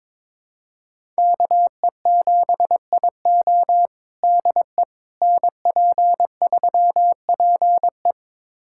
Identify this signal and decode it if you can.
Morse code, heard as a keyed tone, about 22 wpm, 700 Hz. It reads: KE7IO DE NP3PE
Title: USA Callsign DE Callsign - 5 Characters - 22wpm